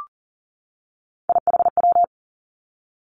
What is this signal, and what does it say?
Morse code, heard as a keyed tone, about 38 wpm, 690 Hz. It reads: IHW
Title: Three Challenging Letter-Number Combinations 40wpm